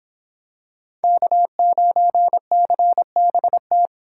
Morse code, heard as a keyed tone, about 25 wpm, 700 Hz. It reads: K9CBT